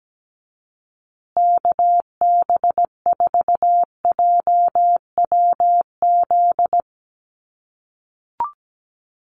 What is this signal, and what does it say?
Morse code, heard as a keyed tone, about 17 wpm, 700 Hz. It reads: KB4JWZ
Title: USA Call Signs Repeated 2x 17wpm